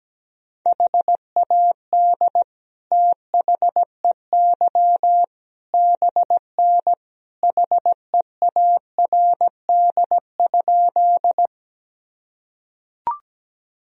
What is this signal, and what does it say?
Morse code, heard as a keyed tone, about 17 wpm, 700 Hz. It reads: HAD THEY BN HEARD?
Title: Sentences from Top 400 Words and Abbreviations 17wpm